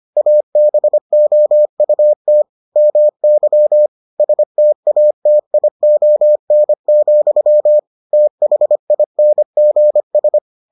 Morse code, heard as a keyed tone, about 25 wpm, 600 Hz. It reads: ABOUT MY STATION, THINGS